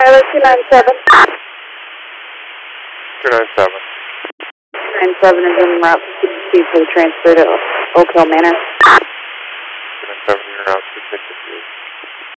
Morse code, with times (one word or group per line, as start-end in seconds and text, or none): none